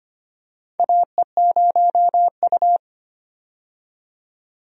Morse code, heard as a keyed tone, about 25 wpm, 700 Hz. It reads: AE0U
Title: USA Call Signs - 4 Characters - 25wpm